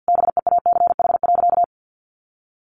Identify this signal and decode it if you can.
Morse code, heard as a keyed tone, about 50 wpm, 700 Hz. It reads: BEACH.
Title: Sentences from Top 1500 Words 50wpm